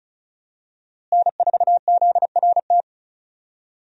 Morse code, heard as a keyed tone, about 36 wpm, 700 Hz. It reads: N4ZRT
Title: USA Call Signs - 5 Characters - Repeated 3x - 35wpm